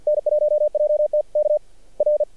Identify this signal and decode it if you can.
Morse code, heard as a keyed tone, about 48 wpm, 600 Hz. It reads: N1JT K R
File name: archived / CRSnet